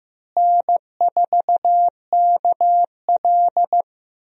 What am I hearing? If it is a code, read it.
Morse code, heard as a keyed tone, about 15 wpm, 700 Hz. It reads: N4KL
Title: USA Call Signs - 4 Characters - Repeated 2x - 15wpm